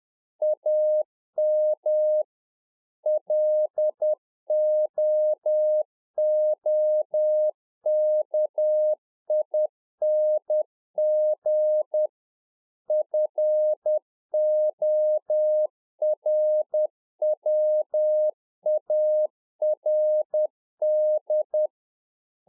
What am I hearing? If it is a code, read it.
Morse code, heard as a keyed tone, about 10 wpm, 610 Hz. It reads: AM LOOKING FORWARD
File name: 2011-04-mp3 / RICK JJ0MPI 10WPM